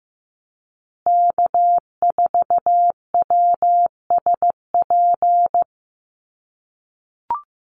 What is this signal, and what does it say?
Morse code, heard as a keyed tone, about 15 wpm, 700 Hz. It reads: K4WSP
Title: USA Call Signs Repeated 2x 15wpm